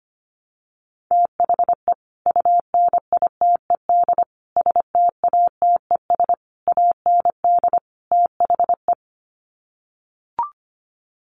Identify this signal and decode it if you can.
Morse code, heard as a keyed tone, about 25 wpm, 700 Hz. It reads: THE UNITED STATES AND THE
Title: Sets of 5 Words 25wpm